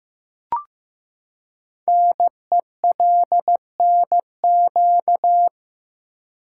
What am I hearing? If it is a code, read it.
Morse code, heard as a keyed tone, about 15 wpm, 700 Hz. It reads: NELNQ
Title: Code Groups - 5 Letters - 15wpm